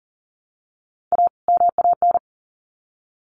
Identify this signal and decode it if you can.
Morse code, heard as a keyed tone, about 40 wpm, 700 Hz. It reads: A MAN
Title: Sets of 2 Words 40wpm